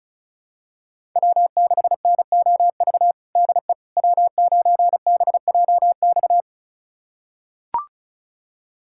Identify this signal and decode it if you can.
Morse code, heard as a keyed tone, about 36 wpm, 700 Hz. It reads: W6NOV DE W9BJX
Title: USA Callsign DE Callsign - 5 Characters - 35wpm